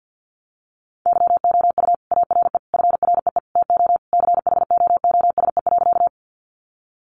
Morse code, heard as a keyed tone, about 50 wpm, 700 Hz. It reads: YOU ARE FREE TO CHOOSE.
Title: Sentences from Top 700 Words 50wpm